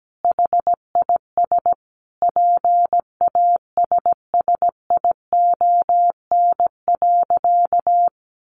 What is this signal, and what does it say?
Morse code, heard as a keyed tone, about 17 wpm, 700 Hz. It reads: HIS PASSION.
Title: Sentences from Top 1300 Words 17wpm